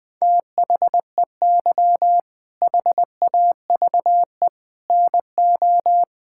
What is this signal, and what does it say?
Morse code, heard as a keyed tone, about 20 wpm, 700 Hz. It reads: THEY HAVE NO
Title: Sentences from Top 1600 Words 20wpm